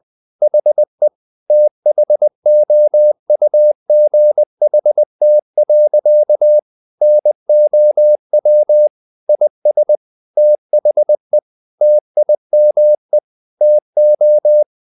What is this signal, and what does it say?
Morse code, heard as a keyed tone, about 20 wpm, 600 Hz. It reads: HE THOUGHT. NOW IS THE TIME TO
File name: the old man and the sea / the old man and the sea part 1 20wpm